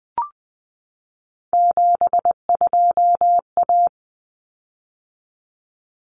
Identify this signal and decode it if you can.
Morse code, heard as a keyed tone, about 20 wpm, 700 Hz. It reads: 72A
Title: Common Cut Numbers - 3-Digit 20wpm